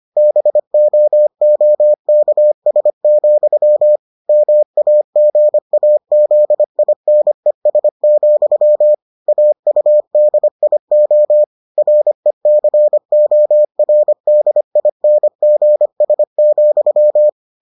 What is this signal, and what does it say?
Morse code, heard as a keyed tone, about 25 wpm, 600 Hz. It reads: BOOKS, MAGAZINES, AUDIO RECORDINGS,